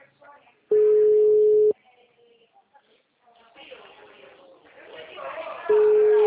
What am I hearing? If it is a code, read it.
Ringback tone (Europe, 425 Hz), with speech.
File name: out- -2003- - - .113